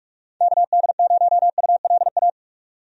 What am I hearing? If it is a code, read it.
Morse code, heard as a keyed tone, about 41 wpm, 700 Hz. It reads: KD0ULA
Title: USA Call Signs - 6 Characters - 45wpm